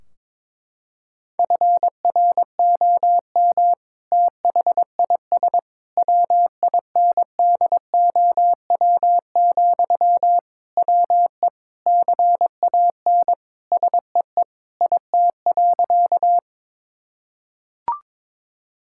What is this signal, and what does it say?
Morse code, heard as a keyed tone, about 22 wpm, 700 Hz. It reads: FROM THIS WINDOW, WE CAN SEE IT.
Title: Sentences from Top 1600 Words 22wpm